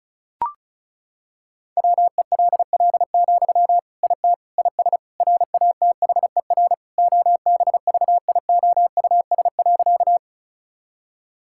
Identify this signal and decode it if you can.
Morse code, heard as a keyed tone, about 36 wpm, 700 Hz. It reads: WELL, IT IS RATHER OBVIOUS.
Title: Sentences from Top 1600 Words 35wpm